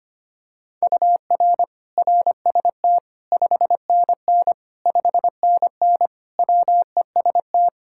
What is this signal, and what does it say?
Morse code, heard as a keyed tone, about 25 wpm, 700 Hz. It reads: UR RST 5NN 5NN WEST
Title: Practice QSO - Exchange 1 - Beginning - 25wpm